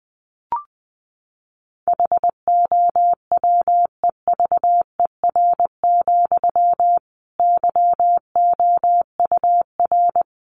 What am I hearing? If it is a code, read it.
Morse code, heard as a keyed tone, about 20 wpm, 700 Hz. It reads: HOWEVER, YOUR
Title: Sentences from Top 1200 Words 20wpm